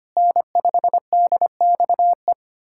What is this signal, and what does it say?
Morse code, heard as a keyed tone, about 25 wpm, 700 Hz. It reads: N5DXE
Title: USA Callsign DE Callsign - 5 Characters - 25wpm